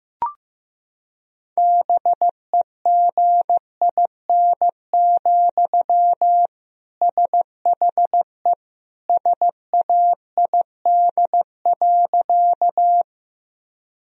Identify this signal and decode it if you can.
Morse code, heard as a keyed tone, about 15 wpm, 700 Hz. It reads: BEGIN, SHE SAID.
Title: Sentences from Top 1000 Words - Review Part 1 - 15wpm